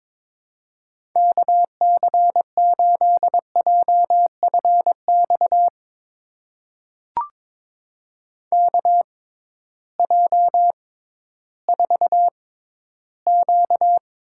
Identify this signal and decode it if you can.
Morse code, heard as a keyed tone, about 22 wpm, spaced as for 13 wpm, 700 Hz. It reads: KC8JFX KJ4Q
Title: USA Call Signs - 6 Characters - 6x Character Spacing - 22wpm